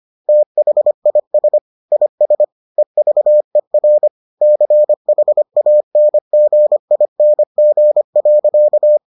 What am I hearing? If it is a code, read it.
Morse code, heard as a keyed tone, about 25 wpm, 600 Hz. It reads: THIS IS EVER CHANGING.